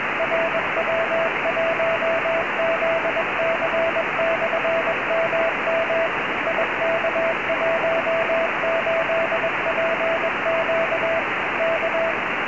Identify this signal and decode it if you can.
Morse code, heard as a keyed tone, about 21 wpm, 660 Hz, with radio noise. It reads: RW1ZC/MM IK18PQ K
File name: ei5fk / rw1zc ik18pq cq